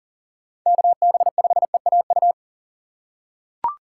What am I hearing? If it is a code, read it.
Morse code, heard as a keyed tone, about 40 wpm, 700 Hz. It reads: KB5EAU
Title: USA Call Signs - 6 Characters - 3x Character Spacing - 40wpm